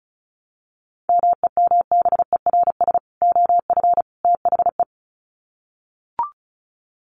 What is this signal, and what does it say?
Morse code, heard as a keyed tone, about 34 wpm, 700 Hz. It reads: MEMBERS OF THE
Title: Sets of 3 Words 35wpm